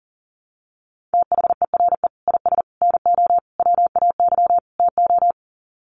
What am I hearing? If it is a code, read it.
Morse code, heard as a keyed tone, about 40 wpm, 700 Hz. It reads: THERE IS NO WAY TO
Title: Sets of 5 Words Encore 40wpm